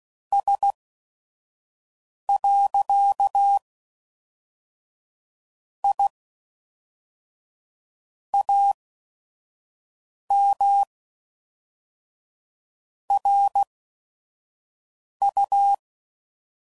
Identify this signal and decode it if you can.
Morse code, heard as a keyed tone, about 16 wpm, spaced as for 5 wpm, 800 Hz. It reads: S. I AM RU
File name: CW Code Groups @5WPM / QSOs / kc7sac